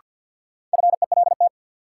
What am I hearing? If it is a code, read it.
Morse code, heard as a keyed tone, about 51 wpm, 700 Hz. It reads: FELT
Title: Sets of 5 Words Encore - New Word Drill - 50wpm